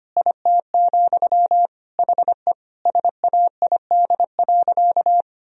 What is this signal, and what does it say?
Morse code, heard as a keyed tone, about 25 wpm, 700 Hz. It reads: IT, HE SAID.
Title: Sentences from Top 1000 Words - Review Part 4 - 25wpm